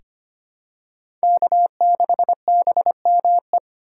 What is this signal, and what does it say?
Morse code, heard as a keyed tone, about 25 wpm, 700 Hz. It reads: K6BME
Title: USA Call Signs - 5 Characters - 3x Character Spacing - 25wpm